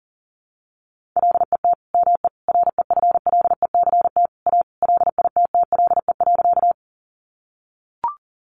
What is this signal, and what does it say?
Morse code, heard as a keyed tone, about 38 wpm, 700 Hz. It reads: LET ME REFLECT A LITTLE.
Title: Sentences from Top 1500 Words 40wpm